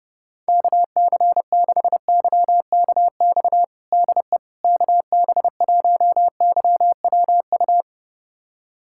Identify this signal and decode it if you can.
Morse code, heard as a keyed tone, about 30 wpm, 700 Hz. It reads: KC6YKX DE KB1YWU
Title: USA Callsign DE Callsign - 6 Characters - 30wpm